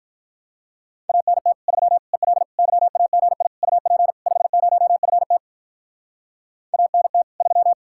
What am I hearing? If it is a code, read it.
Morse code, heard as a keyed tone, about 43 wpm, 700 Hz. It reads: ANT 3 EL YAGI UP 50FT ANT 3
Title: Speed-Racing Common QSO Phrases - 35wpm